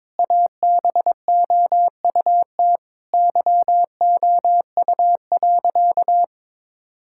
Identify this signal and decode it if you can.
Morse code, heard as a keyed tone, about 22 wpm, 700 Hz. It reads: ABOUT YOU.
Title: Sentences from Top 1000 Words - Review Part 6 - 22wpm